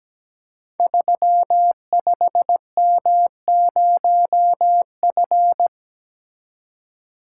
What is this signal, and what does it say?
Morse code, heard as a keyed tone, about 17 wpm, 700 Hz. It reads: 35M0F
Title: Code Groups - 5 Letter:Numbers - 17wpm